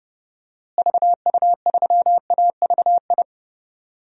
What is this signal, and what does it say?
Morse code, heard as a keyed tone, about 29 wpm, 700 Hz. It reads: VU3AVI
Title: DXCC-100 Call Signs - 6 Characters - Repeated 2x - 30wpm